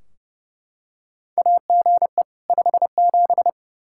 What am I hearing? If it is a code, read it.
Morse code, heard as a keyed tone, about 29 wpm, 700 Hz. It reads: AGE 57
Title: Practice QSO - Exchange 3 - More Info - 30wpm